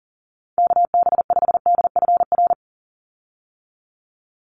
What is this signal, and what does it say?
Morse code, heard as a keyed tone, about 40 wpm, 700 Hz. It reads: KB5DFR
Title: USA Call Signs 40wpm